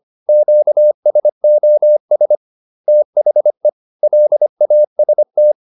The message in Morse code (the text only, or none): QSOS THE LAST